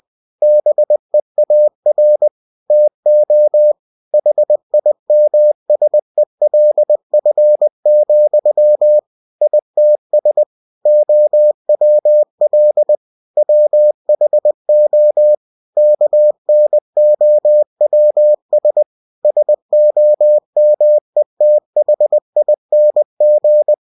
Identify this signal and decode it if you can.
Morse code, heard as a keyed tone, about 20 wpm, 600 Hz. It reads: BEAR TO HIMSELF, ITS OWL WHO KNOWS SOMETHING
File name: cw-books / winnie-the-pooh / winnie-20wpm-page-0028